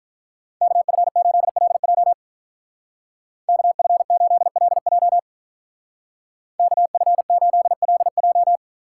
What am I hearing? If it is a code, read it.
Morse code, heard as a keyed tone, about 44 wpm, 700 Hz. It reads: KF8LJ KF8LJ KF8LJ